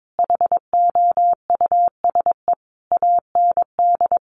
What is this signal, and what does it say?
Morse code, heard as a keyed tone, about 22 wpm, 700 Hz. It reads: HOUSE AND
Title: Sentences from Top 400 Words 22wpm